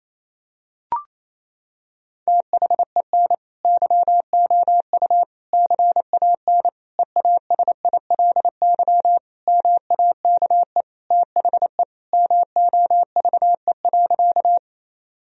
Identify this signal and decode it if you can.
Morse code, heard as a keyed tone, about 28 wpm, 700 Hz. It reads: THEN YOU CAN EASILY MAKE THE MOVE.
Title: Sentences from Top 1000 Words - Review Part 1 - 28wpm